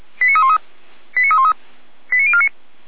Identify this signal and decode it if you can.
Five-tone selective call (ZVEI), heard as quick runs of five tones, tones of about 70 ms. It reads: 89313 89313 89049